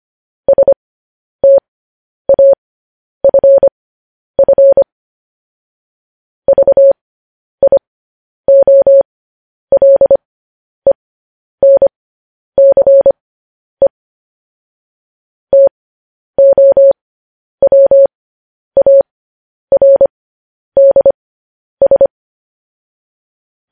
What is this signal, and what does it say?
Morse code, heard as a keyed tone, about 25 wpm, spaced as for 10 wpm, 550 Hz. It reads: STAFF VIOLENCE TOWARDS
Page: news headlines in morse code at 10 wpm